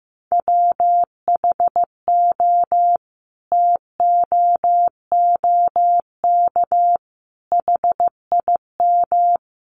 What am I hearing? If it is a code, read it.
Morse code, heard as a keyed tone, about 15 wpm, 700 Hz. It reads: WHO TOOK HIM
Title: Sentences from Top 300 Words 15wpm